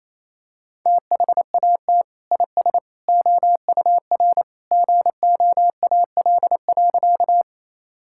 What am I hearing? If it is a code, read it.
Morse code, heard as a keyed tone, about 27 wpm, 700 Hz. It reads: THAT IS OUR GOAL.
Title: Sentences from Top 1000 Words - Review Part 5 - 28wpm